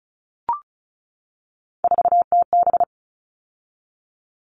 Morse code, heard as a keyed tone, about 34 wpm, 700 Hz. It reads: 4TB
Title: All Cut Numbers - 3-Digit 35wpm